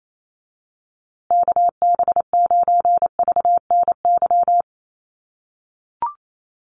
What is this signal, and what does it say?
Morse code, heard as a keyed tone, about 28 wpm, 700 Hz. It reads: KB9VNY
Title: USA Call Signs Repeated 3x 28wpm